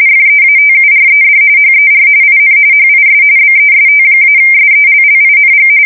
RTTY, about 45.45 baud. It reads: FEATURED HER IN HIS BEST-SELLING
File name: fsk170 / battleshipnj